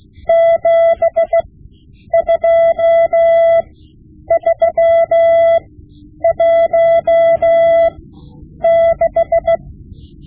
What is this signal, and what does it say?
Morse code, heard as a keyed tone, about 16 wpm, spaced as for 9 wpm, 670 Hz. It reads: 72316
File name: Morse Lessons / Morse Lessons / Tape-5B